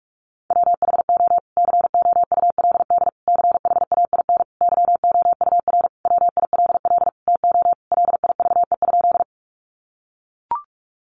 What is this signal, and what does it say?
Morse code, heard as a keyed tone, about 44 wpm, 700 Hz. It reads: WHO COULD CHAIN YOUR WILL TO LIVE?
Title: Sentences from Top 1500 Words 45wpm